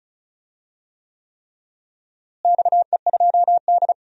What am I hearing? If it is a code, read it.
Morse code, heard as a keyed tone, about 36 wpm, 700 Hz. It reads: XE2D
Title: DXCC-100 Call Signs - 4 Characters - Repeated 3x - 35wpm